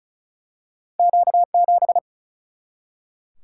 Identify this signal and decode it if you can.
Morse code, heard as a keyed tone, about 36 wpm, 700 Hz. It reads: Q 7
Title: Single-Letter Single-Number 35wpm